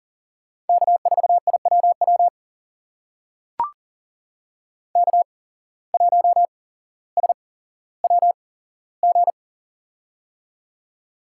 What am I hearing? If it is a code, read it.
Morse code, heard as a keyed tone, about 41 wpm, spaced as for 17 wpm, 700 Hz. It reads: K4IWW K1SWG
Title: USA Call Signs - 5 Characters - 8x Character Spacing - 40wpm